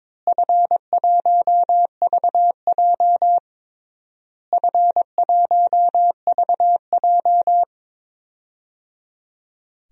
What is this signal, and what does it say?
Morse code, heard as a keyed tone, about 22 wpm, 700 Hz. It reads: F1VJ F1VJ
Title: DXCC-100 Call Signs - 4 Characters - Repeated 3x - 22wpm